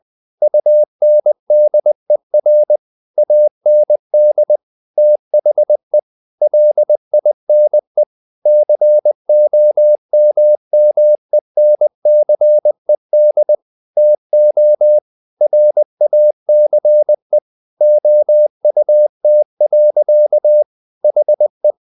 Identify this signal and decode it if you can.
Morse code, heard as a keyed tone, about 20 wpm, 600 Hz. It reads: UNDER AND THE LINE COMMENCED TO RACE OUT. HE
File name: the old man and the sea / the old man and the sea part 1 20wpm